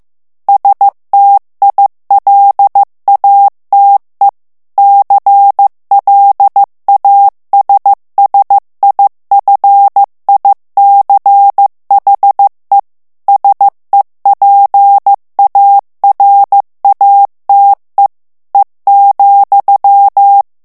Morse code, heard as a keyed tone, about 15 wpm, 800 Hz. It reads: STILATE CLASSIFICHE SEPARATE E,
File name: morse mp3 / ITALIEN texte clair 2 vitesse 14 mots minute